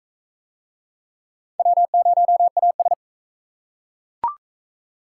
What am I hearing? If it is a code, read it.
Morse code, heard as a keyed tone, about 40 wpm, 700 Hz. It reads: W0AS